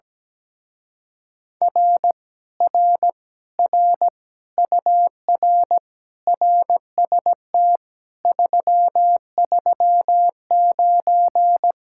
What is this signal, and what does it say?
Morse code, heard as a keyed tone, about 17 wpm, 700 Hz. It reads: R R R UR RST 339
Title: Speed-Racing Common QSO Phrases - 17wpm